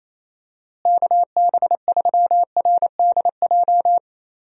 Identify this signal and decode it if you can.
Morse code, heard as a keyed tone, about 28 wpm, 700 Hz. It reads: KB3RDJ
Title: USA Call Signs - 6 Characters - Repeated 2x - 28wpm